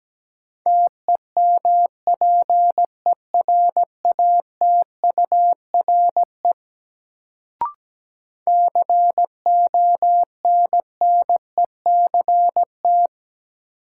Morse code, heard as a keyed tone, about 17 wpm, 700 Hz. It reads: TEMPERATURE CONNECT
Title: Sentences from Top 1500 Words - New Word Drill Alt - 17wpm